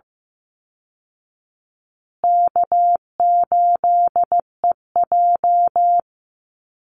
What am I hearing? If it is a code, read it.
Morse code, heard as a keyed tone, about 14 wpm, 700 Hz. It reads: K8EJ
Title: Easy Call Signs Repeated 3x - 15wpm